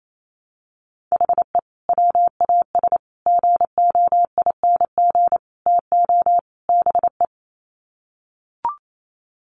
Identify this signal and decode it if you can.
Morse code, heard as a keyed tone, about 28 wpm, 700 Hz. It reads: HE WAS GOING TO BE
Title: Sets of 5 Words 28wpm